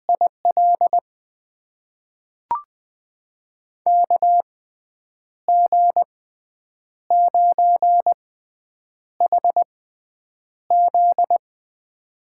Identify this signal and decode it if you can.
Morse code, heard as a keyed tone, about 20 wpm, spaced as for 8 wpm, 700 Hz. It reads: IL KG9HZ
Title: USA Call Signs - 5 Characters - 6x Character Spacing - 20wpm